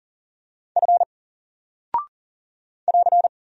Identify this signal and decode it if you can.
Morse code, heard as a keyed tone, about 40 wpm, 700 Hz. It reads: F <AR>
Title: Single Character (Letter:Number:Punctuation:Prosign) 40wpm